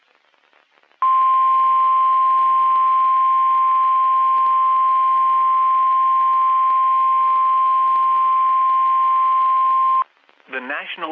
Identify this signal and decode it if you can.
Emergency Alert System (SAME) tones (the 1050 Hz attention tone), with speech. It is heard from 1.0 to 10.1 s.
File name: eas messages / EAS WXM31 Elmira